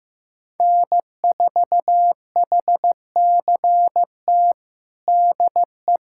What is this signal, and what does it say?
Morse code, heard as a keyed tone, about 16 wpm, 700 Hz. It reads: N4HCT DE
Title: USA Callsign DE Callsign - 5 Characters - 15wpm